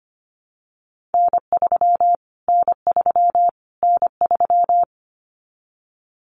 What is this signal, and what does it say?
Morse code, heard as a keyed tone, about 25 wpm, 700 Hz. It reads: N3 N3 N3
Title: Callsign Prefixes for Top 25 DXCC - Repeated 3x - 25wpm